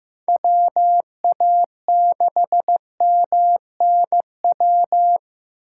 Morse code, heard as a keyed tone, about 15 wpm, 700 Hz. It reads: WA6MNW
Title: USA Call Signs - 6 Characters - 4x Character Spacing - 15wpm